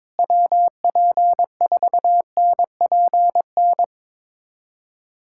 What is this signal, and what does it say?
Morse code, heard as a keyed tone, about 22 wpm, 700 Hz. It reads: WP4NPN
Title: USA Callsign DE Callsign 22wpm